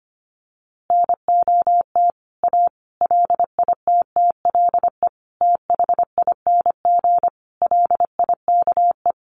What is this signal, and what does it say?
Morse code, heard as a keyed tone, about 25 wpm, 700 Hz. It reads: NOT A LITTLE THING LIKE
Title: A Sound Of Thunder 25wpm